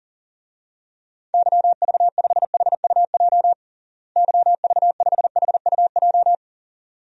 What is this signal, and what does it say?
Morse code, heard as a keyed tone, about 40 wpm, 700 Hz. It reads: YV5HUJ YV5HUJ
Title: DXCC-100 Call Signs - 6 Characters - Repeated 2x - 40wpm